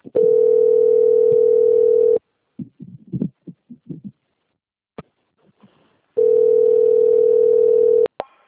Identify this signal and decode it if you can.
Ringback tone (North America), 440 + 480 Hz.
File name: out- - - - - .25639